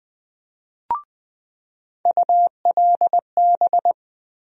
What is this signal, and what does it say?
Morse code, heard as a keyed tone, about 21 wpm, 700 Hz. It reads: ULB
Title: USA Call Sign Postfixes - 3 Letters 20wpm